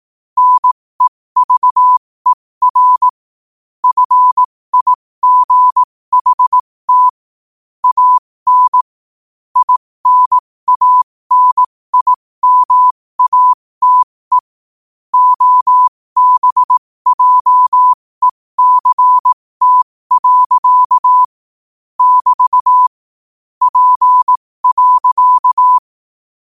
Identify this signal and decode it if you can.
Morse code, heard as a keyed tone, about 18 wpm, spaced as for 15 wpm, 1000 Hz. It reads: NEVER FIGHT AN INANIMATE OBJECT. <BT> P.
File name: qotdfiles / 15wpm